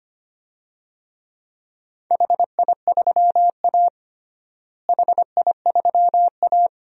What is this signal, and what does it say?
Morse code, heard as a keyed tone, about 25 wpm, 700 Hz. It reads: HI3A HI3A
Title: DXCC-100 Call Signs - 4 Characters - Repeated 2x - 25wpm